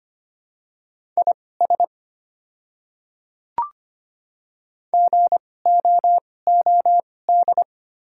Morse code, heard as a keyed tone, about 25 wpm, spaced as for 18 wpm, 690 Hz. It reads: IS GOOD
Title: 2x Character Spacing - Top 200 Words - 25wpm